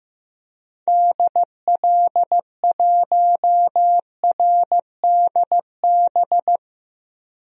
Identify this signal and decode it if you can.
Morse code, heard as a keyed tone, about 15 wpm, 700 Hz. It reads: DL1RDB
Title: DXCC-100 Call Signs - 6 Characters - Repeated 3x - 15wpm